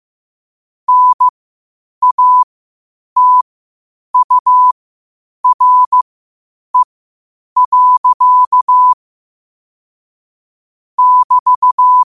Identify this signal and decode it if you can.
Morse code, heard as a keyed tone, about 15 wpm, spaced as for 8 wpm, 1000 Hz. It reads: NATURE. <BT>